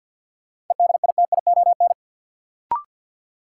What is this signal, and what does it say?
Morse code, heard as a keyed tone, about 51 wpm, 700 Hz. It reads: EDITION
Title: Sets of 5 Words - New Word Drill - 50wpm